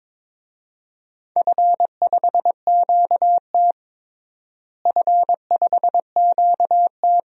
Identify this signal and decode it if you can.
Morse code, heard as a keyed tone, about 22 wpm, 700 Hz. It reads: F5QT F5QT
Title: DXCC-100 Call Signs - 4 Characters - Repeated 3x - 22wpm